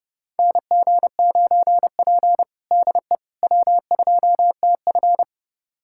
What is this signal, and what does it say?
Morse code, heard as a keyed tone, about 30 wpm, 700 Hz. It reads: NG9P DE W2TF
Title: USA Callsign DE Callsign - 4 Characters - 30wpm